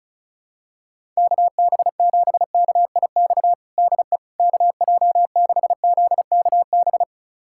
Morse code, heard as a keyed tone, about 36 wpm, 700 Hz. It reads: KB7KIX DE KJ6ZKB
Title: USA Callsign DE Callsign 35wpm